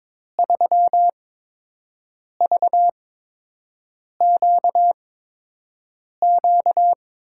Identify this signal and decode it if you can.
Morse code, heard as a keyed tone, about 22 wpm, spaced as for 6 wpm, 700 Hz. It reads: 3VQQ
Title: USA Call Signs - 6 Characters - 8x Character Spacing - 22wpm